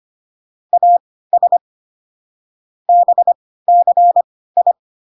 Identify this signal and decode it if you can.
Morse code, heard as a keyed tone, about 25 wpm, spaced as for 16 wpm, 700 Hz. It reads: AS BCI